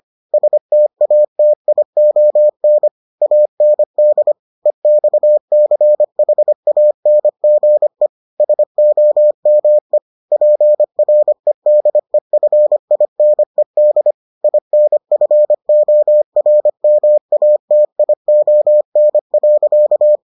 Morse code, heard as a keyed tone, about 25 wpm, 600 Hz. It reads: STATION AND EXCHANGE SOME PREDEFINED INFORMATION.